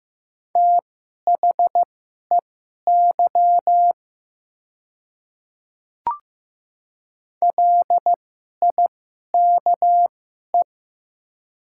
Morse code, heard as a keyed tone, about 15 wpm, spaced as for 11 wpm, 700 Hz. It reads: THEY LIKE
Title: 2x Character Spacing - Top 200 Words - 15wpm